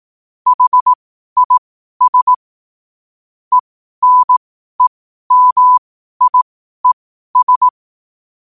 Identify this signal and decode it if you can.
Morse code, heard as a keyed tone, about 18 wpm, spaced as for 12 wpm, 1000 Hz. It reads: HIS ENEMIES
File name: qotdfiles / 12wpm